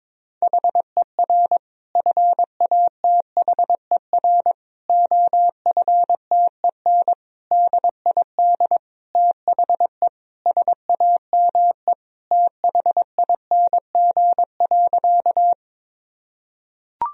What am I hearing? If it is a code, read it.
Morse code, heard as a keyed tone, about 22 wpm, 700 Hz. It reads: HER FATHER OFTEN DID THE SAME THING.
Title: Sentences from Top 1000 Words - Review Part 1 - 22wpm